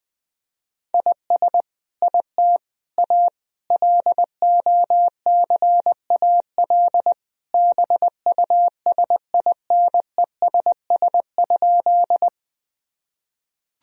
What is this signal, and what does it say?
Morse code, heard as a keyed tone, about 21 wpm, 700 Hz. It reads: IS IT A LOCAL BUSINESS?
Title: Sentences from Top 1000 Words - Review Part 3 - 20wpm